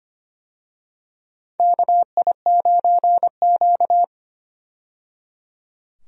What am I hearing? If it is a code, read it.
Morse code, heard as a keyed tone, about 25 wpm, 700 Hz. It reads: KI9Q
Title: USA Call Signs - 4 Characters - 25wpm